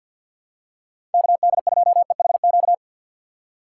Morse code, heard as a keyed tone, about 51 wpm, 700 Hz. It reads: KD2EHQ
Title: USA Call Signs - 6 Characters - 4x Character Spacing - 50wpm